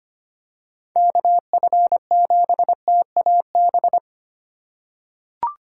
Morse code, heard as a keyed tone, about 25 wpm, 700 Hz. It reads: KF7TAB
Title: USA Call Signs - 6 Characters - 2x Character Spacing - 25wpm